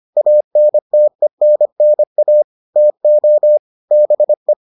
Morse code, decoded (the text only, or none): ANTENNA TO BE